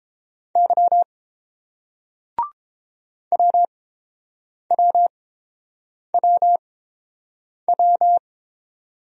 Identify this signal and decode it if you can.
Morse code, heard as a keyed tone, about 29 wpm, spaced as for 7 wpm, 700 Hz. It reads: Y WWWW